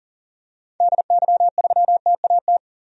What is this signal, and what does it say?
Morse code, heard as a keyed tone, about 40 wpm, 700 Hz. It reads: DY3TAT
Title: DXCC-100 Call Signs - 6 Characters - 40wpm